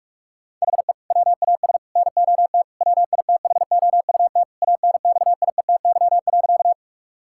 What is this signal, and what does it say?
Morse code, heard as a keyed tone, about 46 wpm, 700 Hz. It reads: HE WAS NOT WITHOUT ANXIETY.
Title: Sentences from Top 1600 Words 45wpm